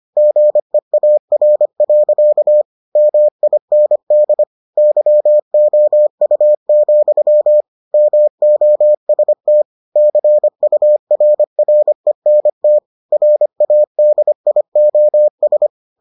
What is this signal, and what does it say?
Morse code, heard as a keyed tone, about 25 wpm, 600 Hz. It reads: GEAR. MIND YOU, MOST CURRENT RADIOS